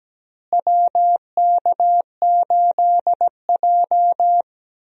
Morse code, heard as a keyed tone, about 17 wpm, 700 Hz. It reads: WK8J
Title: USA Call Signs - 4 Characters - Repeated 2x - 17wpm